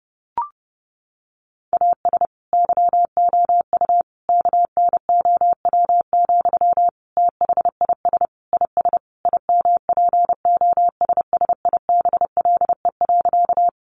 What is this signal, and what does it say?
Morse code, heard as a keyed tone, about 28 wpm, 700 Hz. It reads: AS YOU KNOW, THIS IS IMPOSSIBLE.
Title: Sentences from Top 1100 Words 30wpm